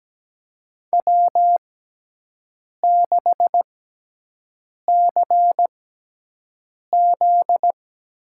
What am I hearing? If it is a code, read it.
Morse code, heard as a keyed tone, about 17 wpm, spaced as for 6 wpm, 700 Hz. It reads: W6CZ
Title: USA Call Signs - 6x Character Spacing - 17wpm